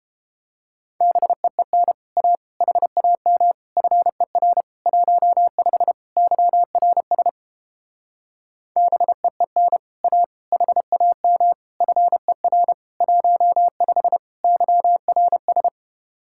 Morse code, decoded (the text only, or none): BEEN A HAM FER 15 YRS BEEN A HAM FER 15 YRS